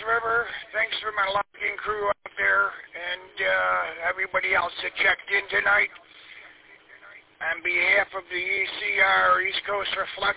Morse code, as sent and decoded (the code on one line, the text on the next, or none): none